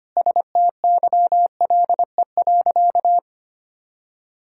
Morse code, heard as a keyed tone, about 25 wpm, 700 Hz. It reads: STYLE.
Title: Sentences from Top 1000 Words - Review Part 5 - 25wpm